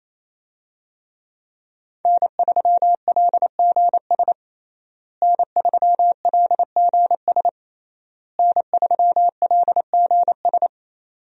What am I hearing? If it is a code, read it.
Morse code, heard as a keyed tone, about 28 wpm, 700 Hz. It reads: N3LGS N3LGS N3LGS
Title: USA Call Signs - 5 Characters - Repeated 3x - 28wpm